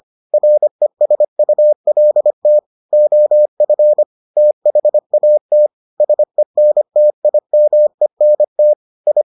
Morse code, heard as a keyed tone, about 25 wpm, 600 Hz. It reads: RESULT OF THAT SENTIMENT I